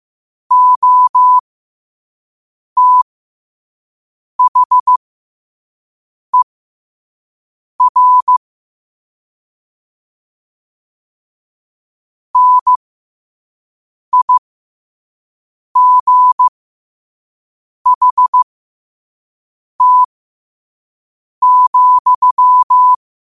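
Morse code, heard as a keyed tone, about 15 wpm, spaced as for 5 wpm, 1000 Hz. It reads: OTHER NIGHT,
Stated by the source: Morse code sent at 5 wpm.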